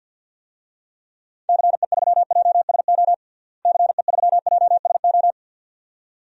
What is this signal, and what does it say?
Morse code, heard as a keyed tone, about 51 wpm, 700 Hz. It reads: CE3JSO CE3JSO
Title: DXCC-100 Call Signs - 6 Characters - Repeated 2x - 50wpm